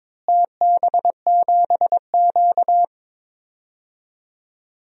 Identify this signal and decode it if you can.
Morse code, heard as a keyed tone, about 22 wpm, 700 Hz. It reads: TB7Q
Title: DXCC-100 Call Signs - 4 Characters - Repeated 2x - 22wpm